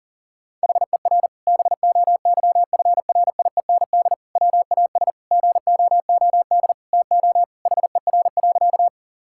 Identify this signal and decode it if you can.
Morse code, heard as a keyed tone, about 37 wpm, 700 Hz. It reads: HER BOYFRIEND WAS GOOD TO HER.
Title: Sentences from Top 1000 Words - Review Part 5 - 40wpm